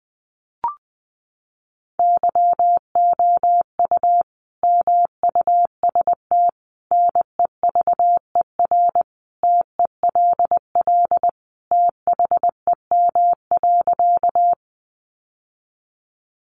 Morse code, heard as a keyed tone, about 20 wpm, 700 Hz. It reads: YOU MUST NEVER TELL THEM.
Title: Sentences from Top 900 Words 20wpm